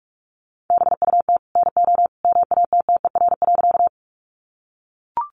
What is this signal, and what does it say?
Morse code, heard as a keyed tone, about 44 wpm, 700 Hz. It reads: BUT NO MATTER.
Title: Sentences from Top 700 Words 45wpm